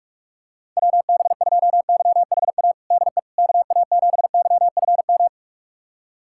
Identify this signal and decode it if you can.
Morse code, heard as a keyed tone, about 46 wpm, 700 Hz. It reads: WB2YHA DE KA7YFM
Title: USA Callsign DE Callsign - 6 Characters - 45wpm